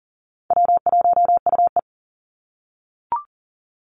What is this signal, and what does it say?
Morse code, heard as a keyed tone, about 40 wpm, 700 Hz. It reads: W1UE
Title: Call Signs - 40wpm